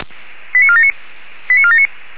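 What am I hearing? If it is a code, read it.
Five-tone selective call (ZVEI), heard as quick runs of five tones, tones of about 70 ms. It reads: 89479 89479
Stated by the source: ZVEI five-tone selective call received on SDR, Feuerwehr Orlamünde audio archive.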